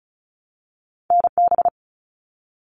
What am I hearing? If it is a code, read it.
Morse code, heard as a keyed tone, about 36 wpm, 700 Hz. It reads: NB
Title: Canadian Provinces and Territories 35wpm